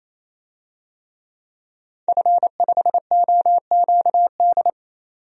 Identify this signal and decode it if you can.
Morse code, heard as a keyed tone, about 28 wpm, 700 Hz. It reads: F5OQD
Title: DXCC-100 Call Signs - 5 Characters - 28wpm